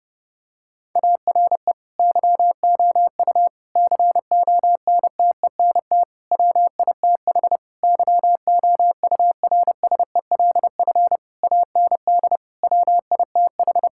Morse code, heard as a keyed tone, about 30 wpm, 700 Hz. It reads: ARE YOU CONTENT WITH YOURSELF AND WITH
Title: Sentences from Top 1000 Words - Review Part 6 - 30wpm